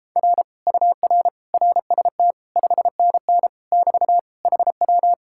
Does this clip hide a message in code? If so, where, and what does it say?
Yes: Morse code from 0.2 to 5.1 s, heard as a keyed tone, about 29 wpm, 700 Hz. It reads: R UR RST 5NN <BT> HW